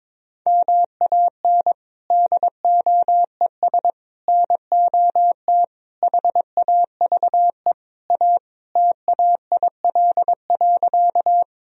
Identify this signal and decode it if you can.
Morse code, heard as a keyed tone, about 22 wpm, 700 Hz. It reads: MAN DOES NOT HAVE A TAIL.
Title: Sentences from Top 1000 Words - Review Part 3 - 22wpm